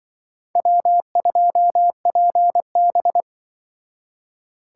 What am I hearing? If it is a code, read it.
Morse code, heard as a keyed tone, about 25 wpm, 700 Hz. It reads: W2PB
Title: Speed Racing - USA 1x2 Call Signs 20wpm